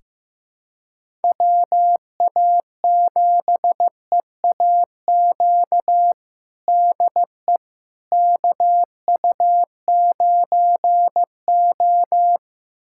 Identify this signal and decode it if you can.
Morse code, heard as a keyed tone, about 15 wpm, 700 Hz. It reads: WA7EAQ DE KU9O
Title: USA Callsign DE Callsign 15wpm